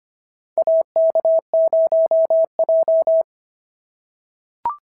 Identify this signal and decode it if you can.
Morse code, heard as a keyed tone, about 25 wpm, 650 Hz. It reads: AK0J
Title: SOTA QSO Practice 25wpm